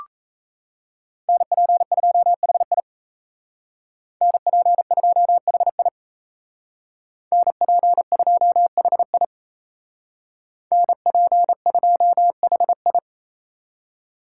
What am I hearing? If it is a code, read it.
Morse code, heard as a keyed tone, about 36 wpm, 700 Hz. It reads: NP2HI NP2HI NP2HI NP2HI